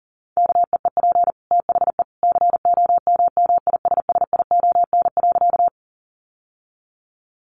Morse code, heard as a keyed tone, about 40 wpm, 700 Hz. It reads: KEEP THE COMMISSION.
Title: Sentences from Top 1400 Words 40wpm